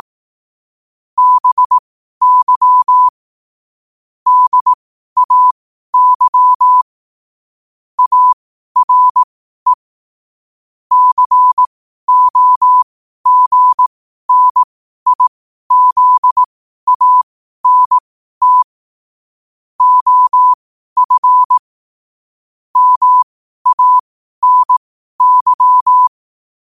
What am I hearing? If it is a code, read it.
Morse code, heard as a keyed tone, about 18 wpm, spaced as for 12 wpm, 1000 Hz. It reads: BY DAY ARE COGNIZANT OF MANY
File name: qotdfiles / 12wpm